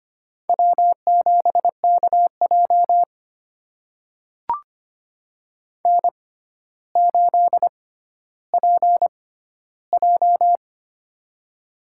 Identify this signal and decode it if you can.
Morse code, heard as a keyed tone, about 25 wpm, spaced as for 13 wpm, 700 Hz. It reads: W7KJ N8PJ
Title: USA Call Signs - 4 Characters - 6x Character Spacing - 25wpm